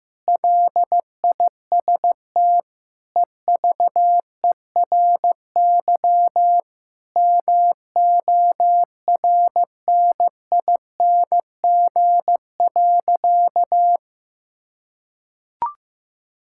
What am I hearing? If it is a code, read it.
Morse code, heard as a keyed tone, about 15 wpm, 700 Hz. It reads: LIST EVERY MORNING.
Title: Sentences from Top 1000 Words - Review Part 6 - 15wpm